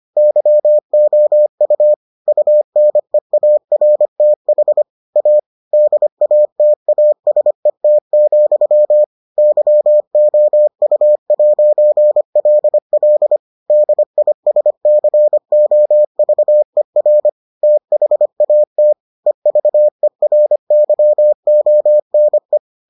Morse code, heard as a keyed tone, about 25 wpm, 600 Hz. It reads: YOU UNEARTH A DATASET, YOU'LL DISCOVER THAT EVERYONE